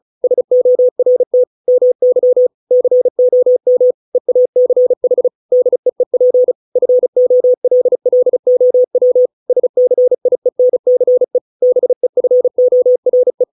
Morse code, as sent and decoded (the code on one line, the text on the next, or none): ... --- .-. - / -- -.-- / -.-. --- -- / . .- -.-. .... / -.. . .--. / ..-. --- .-.. .-.. --- .-- / ... -.-. .. -. -.-. / -... ..-. --- .-. .
SORT MY COM EACH DEEP FOLLOW SCIENCE BEFORE